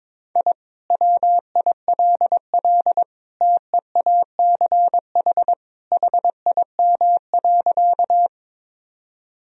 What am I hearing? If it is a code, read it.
Morse code, heard as a keyed tone, about 22 wpm, 700 Hz. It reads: I WILL TEACH HIM.
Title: Sentences from Top 1000 Words - Review Part 2 - 22wpm